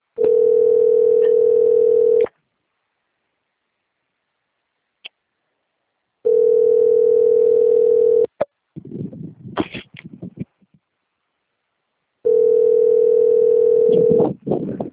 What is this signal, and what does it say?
Ringback tone (North America), 440 + 480 Hz.
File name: rg-1600-unknown- - - .23945